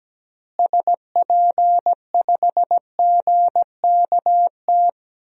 Morse code, heard as a keyed tone, about 17 wpm, 700 Hz. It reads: SP5GKT
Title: DXCC-100 Call Signs - 6 Characters - Repeated 3x - 17wpm